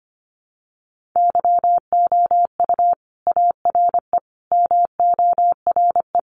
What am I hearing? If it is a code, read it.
Morse code, heard as a keyed tone, about 25 wpm, 700 Hz. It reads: YOU ARE MORE
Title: Sentences from Top 1300 Words 25wpm